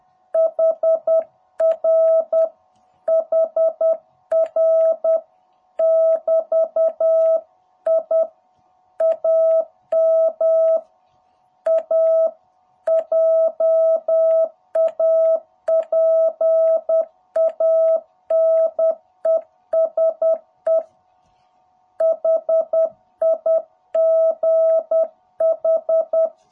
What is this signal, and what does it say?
Morse code, heard as a keyed tone, about 10 wpm, 660 Hz. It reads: HR HR <BT> I AM A JAPANESE HIGH
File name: jh8cbh / 2021cw